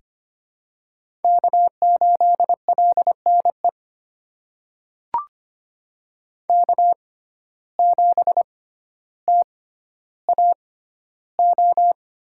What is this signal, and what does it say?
Morse code, heard as a keyed tone, about 25 wpm, spaced as for 13 wpm, 700 Hz. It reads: K8LNE K7TAO
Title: USA Call Signs - 5 Characters - 6x Character Spacing - 25wpm